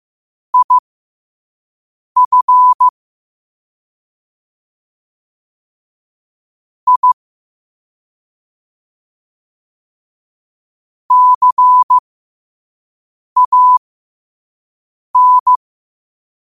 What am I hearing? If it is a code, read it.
Morse code, heard as a keyed tone, about 15 wpm, spaced as for 5 wpm, 1000 Hz. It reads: IF I CAN
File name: qotdfiles / 5wpm